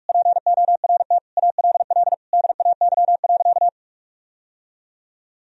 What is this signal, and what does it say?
Morse code, heard as a keyed tone, about 46 wpm, 700 Hz. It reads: PORT ALL DAY.
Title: Sentences from Top 1000 Words - Review Part 1 - 45wpm